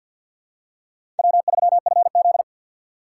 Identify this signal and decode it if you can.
Morse code, heard as a keyed tone, about 49 wpm, 700 Hz. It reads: W3FZ
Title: USA Call Signs - 4 Characters - 3x Character Spacing - 50wpm